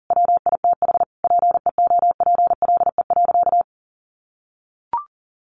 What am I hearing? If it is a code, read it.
Morse code, heard as a keyed tone, about 40 wpm, 700 Hz. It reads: WITH PEOPLE.
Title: Sentences from Top 1100 Words 40wpm